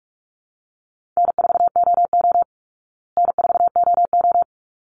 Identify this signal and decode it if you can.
Morse code, heard as a keyed tone, about 44 wpm, 700 Hz. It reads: N4OO N4OO
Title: Call Signs Repeated 3x - 45wpm